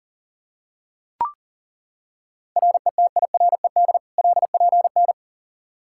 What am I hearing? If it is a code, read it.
Morse code, heard as a keyed tone, about 40 wpm, 700 Hz. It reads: RETIRED LPN
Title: Practice QSO - Exchange 3 - More Info - 40wpm